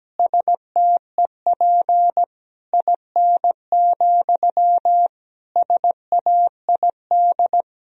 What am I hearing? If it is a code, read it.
Morse code, heard as a keyed tone, about 17 wpm, 700 Hz. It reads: STEP IN, SAID
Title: Sentences from Top 1000 Words - Review Part 2 - 17wpm